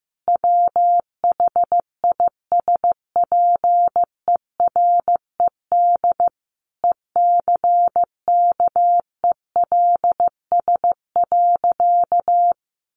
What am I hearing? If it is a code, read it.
Morse code, heard as a keyed tone, about 15 wpm, 700 Hz. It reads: WHISPERED ECKELS.
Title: A Sound Of Thunder 15wpm